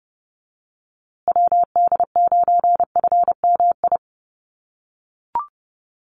Morse code, heard as a keyed tone, about 30 wpm, 700 Hz. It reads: WD9FMI
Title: USA Call Signs Repeated 3x 30wpm